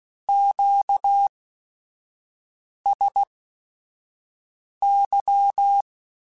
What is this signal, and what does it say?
Morse code, heard as a keyed tone, about 16 wpm, spaced as for 5 wpm, 800 Hz. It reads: QSY
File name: Q Calls / Q Calls-mp3s / Q Calls Twice @5